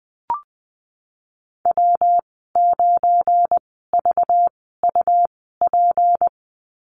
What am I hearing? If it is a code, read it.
Morse code, heard as a keyed tone, about 20 wpm, spaced as for 15 wpm, 700 Hz. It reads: W9VUP
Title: 193 Lesson 31g - Letter v - Call Signs 20wpm